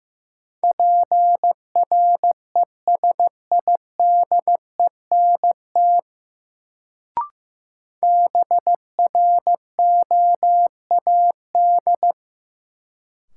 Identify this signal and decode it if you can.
Morse code, heard as a keyed tone, about 16 wpm, 700 Hz. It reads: PRESIDENT BROAD
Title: Sentences from Top 1000 Words - New Word Drill - 15wpm